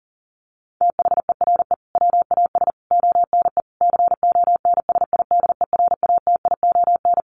Morse code, heard as a keyed tone, about 40 wpm, 700 Hz. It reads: THERE WAS ONE CONSIDERATION
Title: Sentences from Top 1500 Words 40wpm